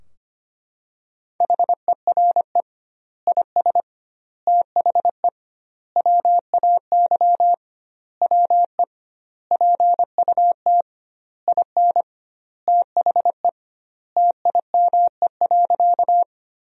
Morse code, heard as a keyed tone, about 25 wpm, 700 Hz. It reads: HERE IS THE WAY WE PUT IN THE TIME.
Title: 2x Word-Spacing - Sentences from Top 200 Words - 25wpm